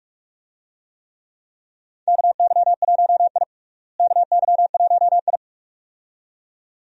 Morse code, heard as a keyed tone, about 46 wpm, 700 Hz. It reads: KY1I KY1I
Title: USA Call Signs - 4 Characters - Repeated 2x - 45wpm